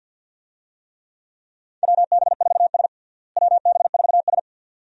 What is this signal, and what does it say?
Morse code, heard as a keyed tone, about 52 wpm, 700 Hz. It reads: WB4S WB4S
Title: USA Call Signs - 4 Characters - Repeated 2x - 50wpm